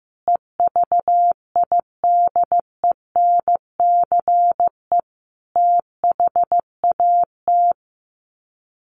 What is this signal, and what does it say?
Morse code, heard as a keyed tone, about 15 wpm, 700 Hz. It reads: EVIDENCE THAT
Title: Sets of 5 Words Encore 15wpm